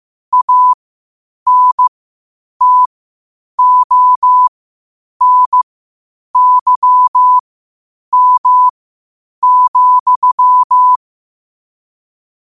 Morse code, heard as a keyed tone, about 15 wpm, spaced as for 8 wpm, 1000 Hz. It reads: ANTONYM,
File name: 8wpm